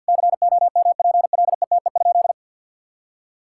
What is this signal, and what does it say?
Morse code, heard as a keyed tone, about 50 wpm, 700 Hz. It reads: COMPLETE?
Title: Sentences from Top 1000 Words - Review Part 6 - 50wpm